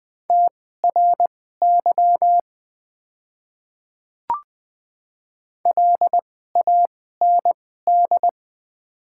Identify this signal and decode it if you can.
Morse code, heard as a keyed tone, about 20 wpm, spaced as for 15 wpm, 700 Hz. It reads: TRY LAND
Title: 2x Character Spacing - Top 300 Words - 20wpm